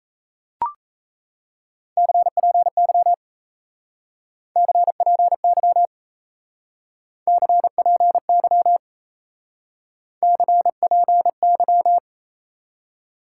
Morse code, heard as a keyed tone, about 38 wpm, 700 Hz. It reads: CPY CPY CPY CPY